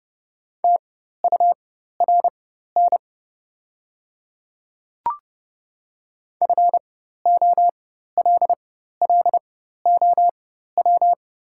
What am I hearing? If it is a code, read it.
Morse code, heard as a keyed tone, about 30 wpm, spaced as for 14 wpm, 700 Hz. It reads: TURN FOLLOW
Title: 4x Character Spacing - Top 200 Words - 30wpm